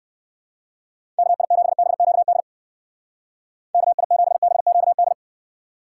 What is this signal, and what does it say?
Morse code, heard as a keyed tone, about 55 wpm, 700 Hz. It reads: KI7BQD KI7BQD
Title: Speed Racing - USA 2x3 Call Signs 45wpm